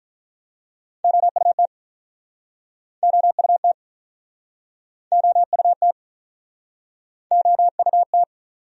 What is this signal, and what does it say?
Morse code, heard as a keyed tone, about 51 wpm, 700 Hz. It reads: OUT OUT OUT OUT